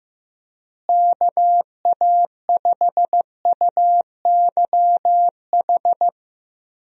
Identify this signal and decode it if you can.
Morse code, heard as a keyed tone, about 15 wpm, 700 Hz. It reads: KA5UYH
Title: USA Call Signs - 6 Characters - Repeated 3x - 15wpm